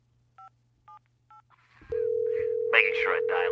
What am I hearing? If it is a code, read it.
DTMF, heard as telephone touch tones, tones of about 120 ms, with speech. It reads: 578